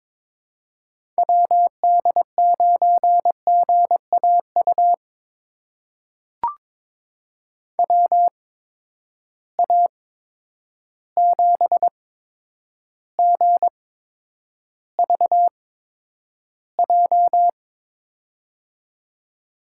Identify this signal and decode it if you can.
Morse code, heard as a keyed tone, about 22 wpm, spaced as for 9 wpm, 700 Hz. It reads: WD9GAU WA7GVJ
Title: USA Call Signs - 6 Characters - 8x Character Spacing - 22wpm